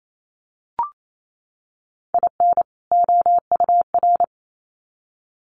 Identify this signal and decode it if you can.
Morse code, heard as a keyed tone, about 28 wpm, 700 Hz. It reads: IN OUR
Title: Sets of 2 Words 28wpm